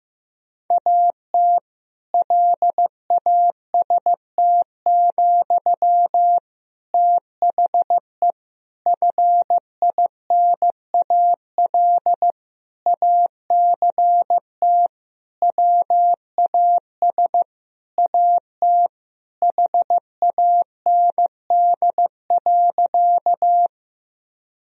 Morse code, heard as a keyed tone, about 16 wpm, 700 Hz. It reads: AT LAST, THE FINAL ACT WAS AT HAND.
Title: Sentences from Top 1000 Words - Review Part 2 - 15wpm